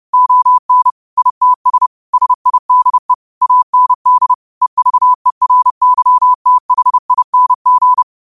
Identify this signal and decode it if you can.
Morse code, heard as a keyed tone, about 30 wpm, 1000 Hz. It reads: ON ITS SIDE AND EVERYTHING
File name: qotdfiles / 30wpm